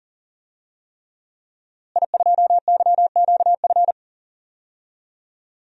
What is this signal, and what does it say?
Morse code, heard as a keyed tone, about 37 wpm, 700 Hz. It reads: I2YQF
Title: DXCC-100 Call Signs - 5 Characters - Repeated 3x - 40wpm